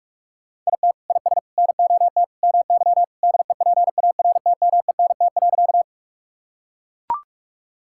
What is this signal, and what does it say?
Morse code, heard as a keyed tone, about 41 wpm, 700 Hz. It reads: IT IS NOT MY DEPARTMENT.
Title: Sentences from Top 1000 Words - Review Part 5 - 45wpm